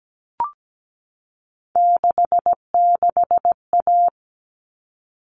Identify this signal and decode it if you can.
Morse code, heard as a keyed tone, about 17 wpm, 700 Hz. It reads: 66A
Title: All Cut Numbers - 3-Digit 17wpm